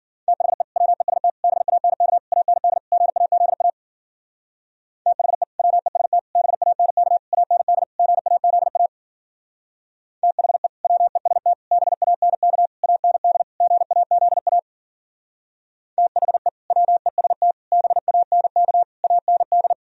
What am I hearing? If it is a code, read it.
Morse code, heard as a keyed tone, about 52 wpm, 700 Hz. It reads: THE WEST BANK AND GAZA THE WEST BANK AND GAZA THE WEST BANK AND GAZA THE WEST BANK AND